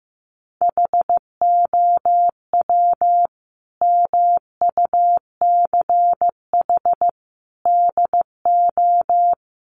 Morse code, heard as a keyed tone, about 15 wpm, 700 Hz. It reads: HOW MUCH DO
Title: Sentences from Top 200 Words 15wpm